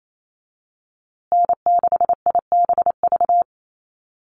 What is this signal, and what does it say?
Morse code, heard as a keyed tone, about 28 wpm, 700 Hz. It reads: N6IBV
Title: USA Call Signs Repeated 2x 28wpm